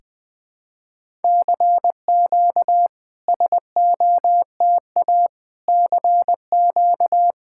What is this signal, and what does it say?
Morse code, heard as a keyed tone, about 20 wpm, 700 Hz. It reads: CQ SOTA CQ
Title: SOTA QSO Practice 20wpm